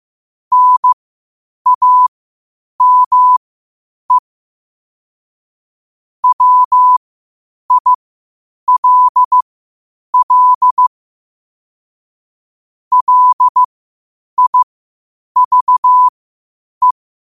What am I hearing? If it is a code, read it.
Morse code, heard as a keyed tone, about 15 wpm, spaced as for 8 wpm, 1000 Hz. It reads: NAME WILL LIVE
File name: qotdfiles / 8wpm